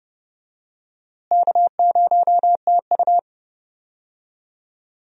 Morse code, heard as a keyed tone, about 30 wpm, 700 Hz. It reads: K0TU
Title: USA Call Signs - 4 Characters - Repeated 2x - 30wpm